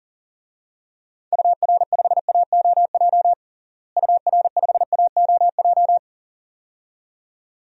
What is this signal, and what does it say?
Morse code, heard as a keyed tone, about 40 wpm, 700 Hz. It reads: UR5AOJ UR5AOJ
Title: DXCC-100 Call Signs - 6 Characters - Repeated 2x - 40wpm